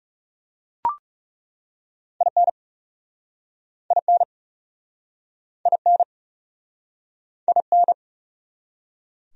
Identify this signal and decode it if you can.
Morse code, heard as a keyed tone, about 37 wpm, 700 Hz. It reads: IN IN IN IN
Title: Speed-Racing Top 500 Words 30wpm